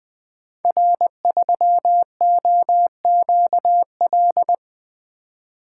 Morse code, heard as a keyed tone, about 20 wpm, 700 Hz. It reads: R3OQL
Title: Code Groups - 5 Letter:Numbers - 20wpm